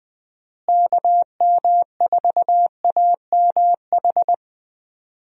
Morse code, heard as a keyed tone, about 20 wpm, 700 Hz. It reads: KM4AMH
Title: USA Call Signs - 6 Characters - Repeated 3x - 20wpm